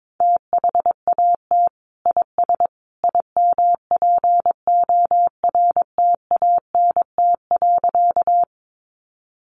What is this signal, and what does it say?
Morse code, heard as a keyed tone, about 22 wpm, 700 Hz. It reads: THAT IS IMPORTANT.
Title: Sentences from Top 600 Words 22wpm